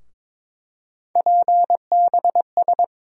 Morse code, heard as a keyed tone, about 21 wpm, 700 Hz. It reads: PBS
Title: USA Call Sign Postfixes - 3 Letters 22wpm